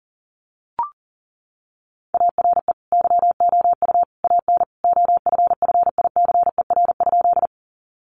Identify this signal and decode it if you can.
Morse code, heard as a keyed tone, about 40 wpm, 700 Hz. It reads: ARE YOU AN OFFICER?
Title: Sentences from Top 1200 Words 40wpm